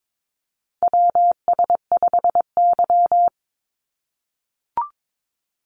Morse code, heard as a keyed tone, about 22 wpm, 700 Hz. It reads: WS5Y
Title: USA Call Signs 22wpm